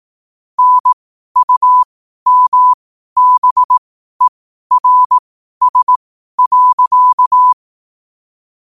Morse code, heard as a keyed tone, about 18 wpm, spaced as for 12 wpm, 1000 Hz. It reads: NUMBERS.
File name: qotdfiles / 12wpm